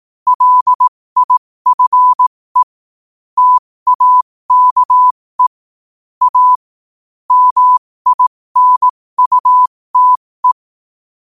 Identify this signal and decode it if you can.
Morse code, heard as a keyed tone, about 18 wpm, spaced as for 15 wpm, 1000 Hz. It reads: LIFE TAKE A MINUTE
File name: qotdfiles / 15wpm